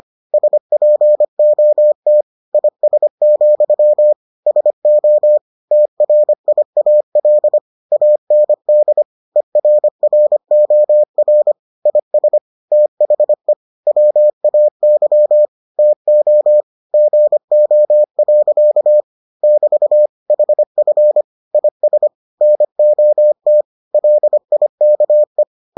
Morse code, heard as a keyed tone, about 25 wpm, 600 Hz. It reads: SPOT IS, SO TRIAL AND ERROR IS THE WAY TO GO. <BT> HF IS NOT LIKE